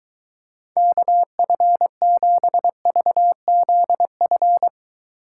Morse code, heard as a keyed tone, about 23 wpm, 700 Hz. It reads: KF7VZF